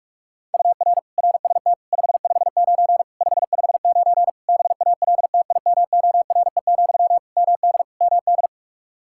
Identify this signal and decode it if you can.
Morse code, heard as a keyed tone, about 46 wpm, 700 Hz. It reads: UR RST 559 559 BALTIMORE, MD MD
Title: Practice QSO - Exchange 1 - Beginning - 45wpm